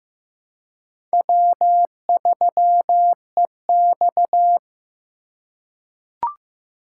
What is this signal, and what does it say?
Morse code, heard as a keyed tone, about 15 wpm, 700 Hz. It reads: W3EX
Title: USA Call Signs - 4 Characters - 3x Character Spacing - 15wpm